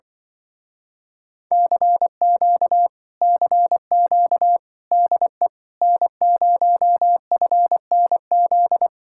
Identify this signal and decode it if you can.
Morse code, heard as a keyed tone, about 24 wpm, 700 Hz. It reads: CQ CQ DE N0FNZ
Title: Speed-Racing Common QSO Phrases Encore - 20wpm